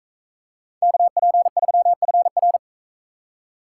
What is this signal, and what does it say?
Morse code, heard as a keyed tone, about 38 wpm, 700 Hz. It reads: KP3FR